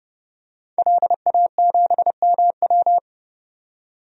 Morse code, heard as a keyed tone, about 31 wpm, 700 Hz. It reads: LA7MW
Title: DXCC-100 Call Signs - 5 Characters - 30wpm